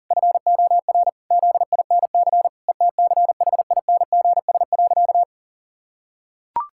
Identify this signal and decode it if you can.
Morse code, heard as a keyed tone, about 40 wpm, 700 Hz. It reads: FOR ZINC ETCHINGS.
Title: Pangram Challenge v2 - 40wpm